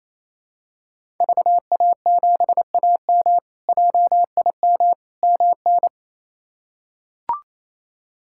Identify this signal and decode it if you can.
Morse code, heard as a keyed tone, about 28 wpm, 700 Hz. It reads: VA7AM JIM MN
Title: K1USN SST QSO Practice - 28wpm